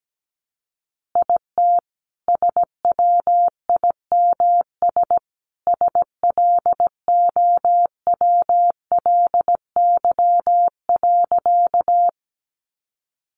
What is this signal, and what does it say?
Morse code, heard as a keyed tone, about 17 wpm, 700 Hz. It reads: IT SWIMS SLOWLY.
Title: Additional Easy Phrases 17wpm